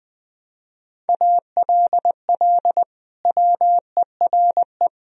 Morse code, heard as a keyed tone, about 20 wpm, 700 Hz. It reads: ALL WERE
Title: Sentences from Top 1000 Words - Review Part 4 - 20wpm